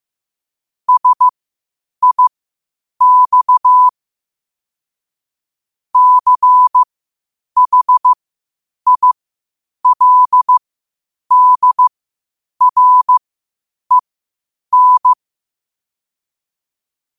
Morse code, heard as a keyed tone, about 15 wpm, spaced as for 8 wpm, 1000 Hz. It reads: SIX CHILDREN
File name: qotdfiles / 8wpm